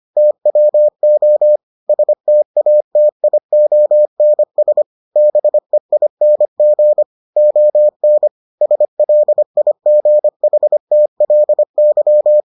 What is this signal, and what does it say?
Morse code, heard as a keyed tone, about 25 wpm, 600 Hz. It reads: TWO STATIONS BEING ON SLIGHTLY